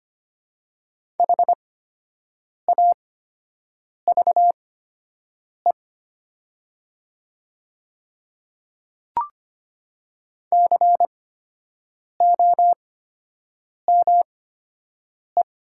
Morse code, heard as a keyed tone, about 25 wpm, spaced as for 7 wpm, 700 Hz. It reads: HAVE COME
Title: 8x Character Spacing - Top 200 Words - 25wpm